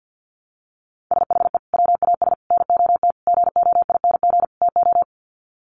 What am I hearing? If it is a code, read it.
Morse code, heard as a keyed tone, about 48 wpm, 700 Hz. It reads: SHE WAS NOT GOING TO
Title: Sets of 5 Words Encore2 50wpm